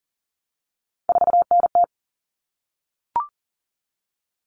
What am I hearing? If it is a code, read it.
Morse code, heard as a keyed tone, about 40 wpm, 700 Hz. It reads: 4NT
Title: Common Cut Numbers - 3-Digit 40wpm